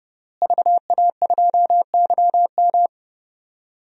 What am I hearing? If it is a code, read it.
Morse code, heard as a keyed tone, about 30 wpm, 700 Hz. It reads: VA2YM
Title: DXCC-100 Call Signs - 5 Characters - Repeated 3x - 30wpm